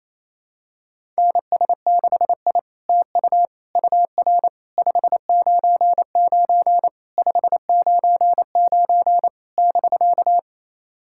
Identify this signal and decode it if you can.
Morse code, heard as a keyed tone, about 29 wpm, 700 Hz. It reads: NS6I TU UR 599 599 <BK>
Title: SOTA QSO Practice 28wpm